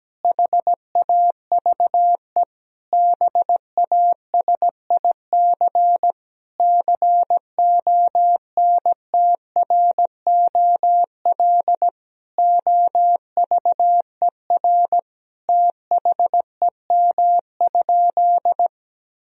Morse code, heard as a keyed tone, about 17 wpm, 700 Hz. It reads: HAVE BASIC CONTROL OVER THEM?
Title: Sentences from Top 1000 Words - Review Part 3 - 17wpm